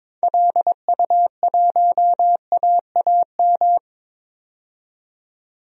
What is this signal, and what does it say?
Morse code, heard as a keyed tone, about 22 wpm, 700 Hz. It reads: LU1AAM
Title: DXCC-100 Call Signs - 6 Characters - 22wpm